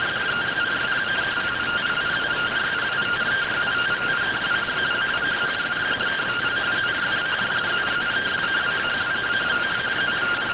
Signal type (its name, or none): two-tone teleprinter signal